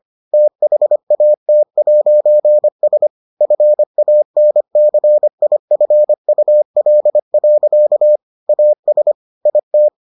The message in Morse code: THAT'S FANCIFUL. AS IT

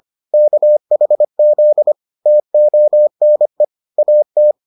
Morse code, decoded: KHZ TONE AT

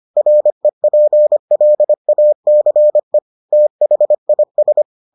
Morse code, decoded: REPLACE THIS

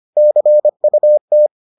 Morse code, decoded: CUT

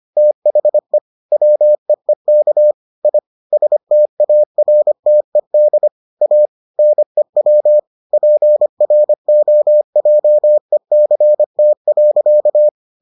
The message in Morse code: THE WEEK I STARTED A NEW PROJECT.

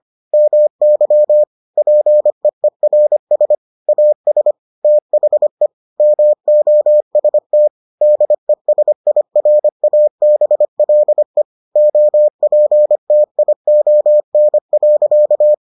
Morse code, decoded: MY PEERS AS THE MOST DESIRABLE OPTION.

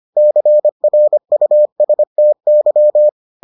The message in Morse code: CRUSTY

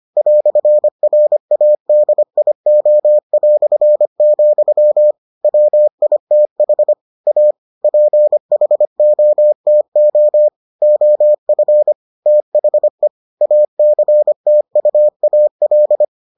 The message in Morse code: "RADIO", WITH A PHOTO OF THE ACTUAL

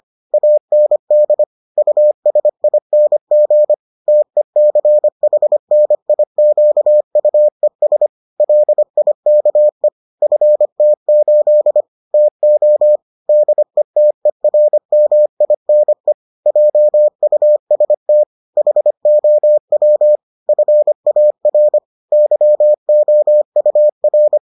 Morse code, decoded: AND USING TECHNIQUES LIKE FT8 TO DETERMINE JUST HOW FAR YOUR

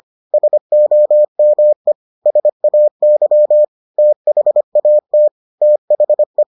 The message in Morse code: SOME SAY THAT THE

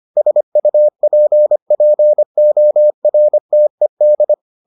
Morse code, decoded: SUPPORTED

